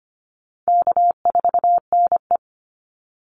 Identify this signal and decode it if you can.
Morse code, heard as a keyed tone, about 25 wpm, 700 Hz. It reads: K4NE
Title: Easy Call Signs - 25wpm